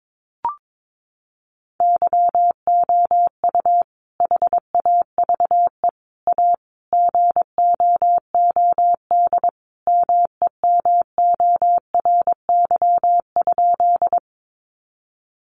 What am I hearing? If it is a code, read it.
Morse code, heard as a keyed tone, about 22 wpm, 700 Hz. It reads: YOU HAVE A GOOD MEMORY?
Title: Sentences from Top 1300 Words 22wpm